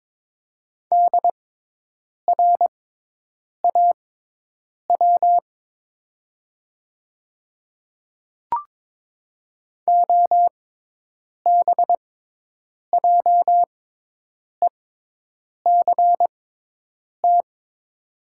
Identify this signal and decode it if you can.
Morse code, heard as a keyed tone, about 23 wpm, spaced as for 8 wpm, 700 Hz. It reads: DRAW OBJECT
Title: 6x Character Spacing - Top 500 Words - 22wpm